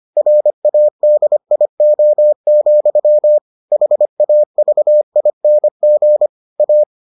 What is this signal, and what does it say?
Morse code, heard as a keyed tone, about 25 wpm, 600 Hz. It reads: RADIO, HAVING A